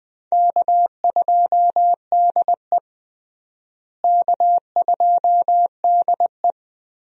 Morse code, heard as a keyed tone, about 21 wpm, 700 Hz. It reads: K2DE K2DE
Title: USA Call Signs - 4 Characters - Repeated 2x - 20wpm